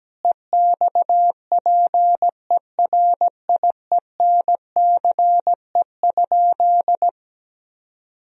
Morse code, heard as a keyed tone, about 17 wpm, 700 Hz. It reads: EXPERIENCE?
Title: Sentences from Top 1000 Words - Review Part 4 - 17wpm